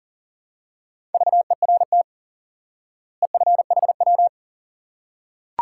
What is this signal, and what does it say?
Morse code, heard as a keyed tone, about 40 wpm, 700 Hz. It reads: VERT EFHW
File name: IQE Course 3 - Lesson 89 - YAGI - Two Elements - 40wpm